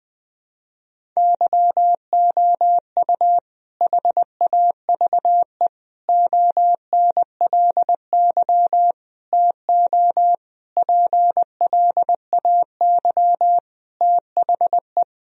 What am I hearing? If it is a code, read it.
Morse code, heard as a keyed tone, about 20 wpm, 700 Hz. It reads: YOU HAVE ONLY TO PLAY THE
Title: Sentences from Top 1000 Words - Review Part 1 - 20wpm